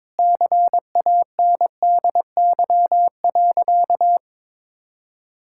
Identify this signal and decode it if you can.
Morse code, heard as a keyed tone, about 22 wpm, 700 Hz. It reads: CANDY.
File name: practice / 22wpm / Sentences with 4-Word Phrases - 22wpm